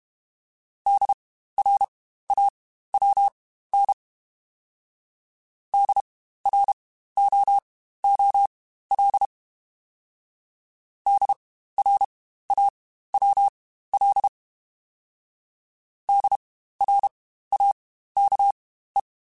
Morse code, heard as a keyed tone, about 31 wpm, spaced as for 15 wpm, 800 Hz. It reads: DRAWN DROOL DRAWL DRAKE